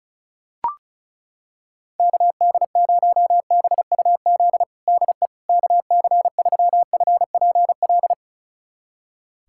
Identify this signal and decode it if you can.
Morse code, heard as a keyed tone, about 36 wpm, 700 Hz. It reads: KD0BUZ DE KC3FPL
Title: USA Callsign DE Callsign - 6 Characters - 35wpm